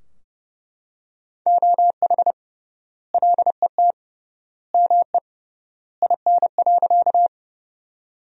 Morse code, heard as a keyed tone, about 30 wpm, 700 Hz. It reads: OH LET ME IN.
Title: 3x Word-Spacing - Sentences from Top 500 Words - 30wpm